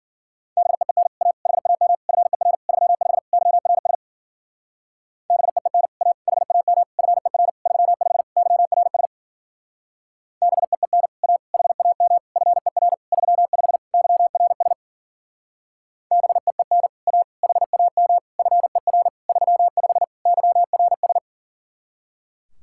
Morse code, heard as a keyed tone, about 51 wpm, 700 Hz. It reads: BEEN A HAM FER 35 YRS BEEN A HAM FER 35 YRS BEEN A HAM FER 35 YRS BEEN A HAM FER 35 YRS